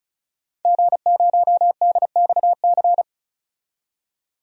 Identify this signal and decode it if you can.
Morse code, heard as a keyed tone, about 36 wpm, 700 Hz. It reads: G0DXC
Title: DXCC-100 Call Signs - 5 Characters - 35wpm